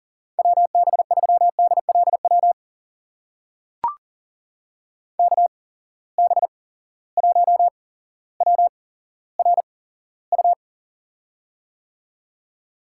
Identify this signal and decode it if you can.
Morse code, heard as a keyed tone, about 40 wpm, spaced as for 17 wpm, 700 Hz. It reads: WB3DLW KB1WRU
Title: USA Call Signs - 6 Characters - 8x Character Spacing - 40wpm